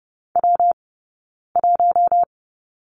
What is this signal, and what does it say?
Morse code, heard as a keyed tone, about 30 wpm, 700 Hz. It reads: W 1
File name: Lesson 19d - Letter w - Two Characters 30wpm